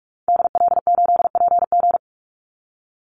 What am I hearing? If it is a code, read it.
Morse code, heard as a keyed tone, about 44 wpm, 700 Hz. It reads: DL8PG
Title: Call Signs - 45wpm